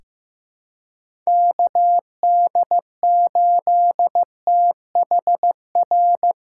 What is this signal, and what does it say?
Morse code, heard as a keyed tone, about 15 wpm, 700 Hz. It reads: KD8THR
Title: USA Callsign DE Callsign - 6 Characters - 15wpm